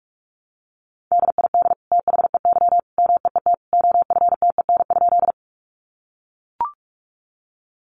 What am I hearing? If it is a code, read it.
Morse code, heard as a keyed tone, about 44 wpm, 700 Hz. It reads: DID THEY MEET OFTEN?
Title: Sentences from Top 1100 Words 45wpm